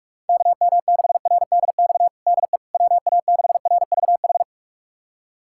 Morse code, heard as a keyed tone, about 46 wpm, 700 Hz. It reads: KM6RDX DE WA6RVH
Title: USA Callsign DE Callsign - 6 Characters - 45wpm